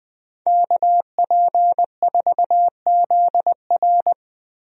Morse code, heard as a keyed tone, about 20 wpm, 700 Hz. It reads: KP4ZR
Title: USA Call Signs - 5 Characters - 20wpm